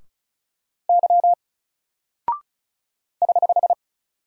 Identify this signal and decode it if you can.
Morse code, heard as a keyed tone, about 36 wpm, 700 Hz. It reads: Y <HH>
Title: Single Character (Letter:Number:Punctuation:Prosign) 35wpm